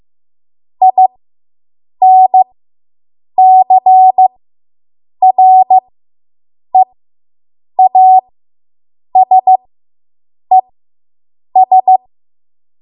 Morse code, heard as a keyed tone, about 15 wpm, spaced as for 7 wpm, 750 Hz. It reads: INCREASES